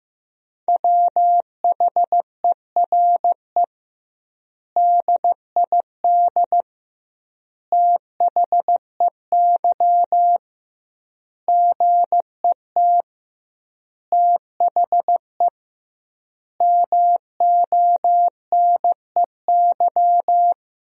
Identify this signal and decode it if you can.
Morse code, heard as a keyed tone, about 15 wpm, 700 Hz. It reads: WHERE DID THEY GET THE MONEY
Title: 2x Word-Spacing - Sentences from Top 400 Words - 15wpm